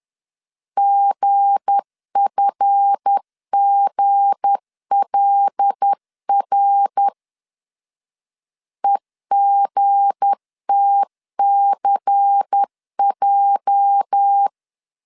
Morse code, heard as a keyed tone, about 10 wpm, 800 Hz. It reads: GFGLR EGTCJ
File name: CW / DET / 1 9 1500C S B 10wpm 4439